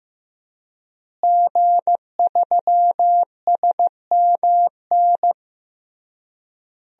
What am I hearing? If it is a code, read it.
Morse code, heard as a keyed tone, about 15 wpm, 700 Hz. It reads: G3SMN
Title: DXCC-100 Call Signs - 5 Characters - Repeated 3x - 15wpm